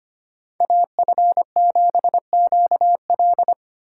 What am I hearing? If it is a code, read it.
Morse code, heard as a keyed tone, about 25 wpm, 700 Hz. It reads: AF7QL
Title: USA Call Signs - 4x Character Spacing - 25wpm